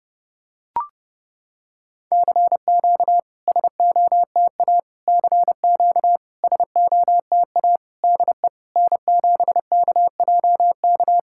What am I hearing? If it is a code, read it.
Morse code, heard as a keyed tone, about 30 wpm, 690 Hz. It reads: CQ SOTA CQ SOTA DE N7KJK
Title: SOTA QSO Practice 30wpm